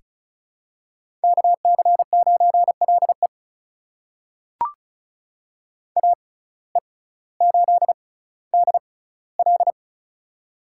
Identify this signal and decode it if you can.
Morse code, heard as a keyed tone, about 36 wpm, spaced as for 18 wpm, 700 Hz. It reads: KC9LE AE8DL
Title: USA Call Signs - 5 Characters - 6x Character Spacing - 35wpm